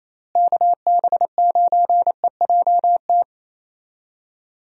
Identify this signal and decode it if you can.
Morse code, heard as a keyed tone, about 28 wpm, 700 Hz. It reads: KB9EJT
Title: USA Call Signs - 6 Characters - Repeated 3x - 28wpm